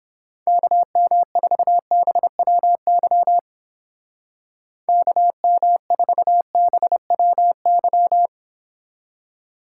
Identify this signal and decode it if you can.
Morse code, heard as a keyed tone, about 29 wpm, 700 Hz. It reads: KM4BWY KM4BWY